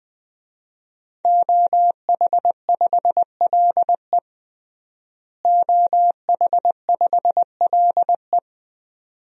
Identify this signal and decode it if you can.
Morse code, heard as a keyed tone, about 20 wpm, 700 Hz. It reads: OH5LE OH5LE
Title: DXCC-100 Call Signs - 5 Characters - Repeated 2x - 20wpm